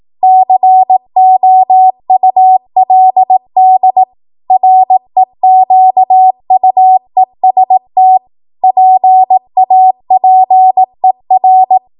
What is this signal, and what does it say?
Morse code, heard as a keyed tone, about 18 wpm, 750 Hz. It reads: COULD REQUEST PAPER